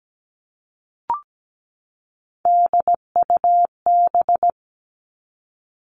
Morse code, heard as a keyed tone, about 17 wpm, 700 Hz. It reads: DUB
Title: Three Challenging Letter-Number Combinations 17wpm